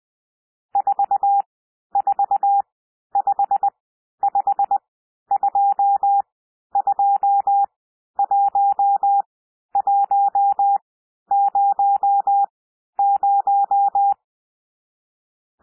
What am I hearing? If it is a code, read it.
Morse code, heard as a keyed tone, about 20 wpm, spaced as for 11 wpm, 800 Hz. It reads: 4455221100
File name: cwmp3 / N1 SOUND NUMBERS 20 wpm